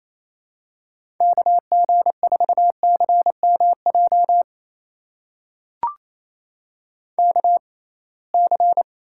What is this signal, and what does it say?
Morse code, heard as a keyed tone, about 28 wpm, spaced as for 21 wpm, 700 Hz. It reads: KG4CMJ KC
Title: USA Call Signs - 6 Characters - 6x Character Spacing - 28wpm